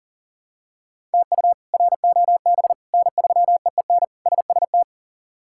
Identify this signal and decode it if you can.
Morse code, heard as a keyed tone, about 40 wpm, 700 Hz. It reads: TU ROB N3EEN SST
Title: K1USN SST QSO Practice - 40wpm